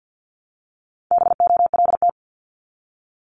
Morse code, heard as a keyed tone, about 50 wpm, 700 Hz. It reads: BOLT
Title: Multi-letter Phonemes 50wpm